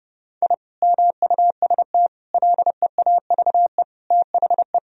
Morse code, heard as a keyed tone, about 30 wpm, 700 Hz. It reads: I MUST LEAVE THE
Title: Sentences from Top 1000 Words - Review Part 2 - 30wpm